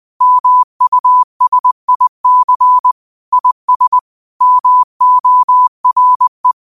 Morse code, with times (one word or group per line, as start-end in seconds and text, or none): 0.2-2.9 MUSIC
3.3-4.0 IS
4.4-6.5 MORE